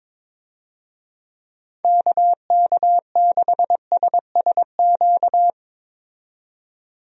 Morse code, heard as a keyed tone, about 22 wpm, 700 Hz. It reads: KK6SSQ
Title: USA Call Signs - 6 Characters - 22wpm